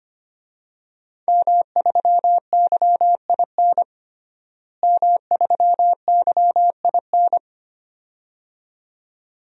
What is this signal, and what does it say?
Morse code, heard as a keyed tone, about 25 wpm, 700 Hz. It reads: M3YIN M3YIN
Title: DXCC-100 Call Signs - 5 Characters - Repeated 2x - 25wpm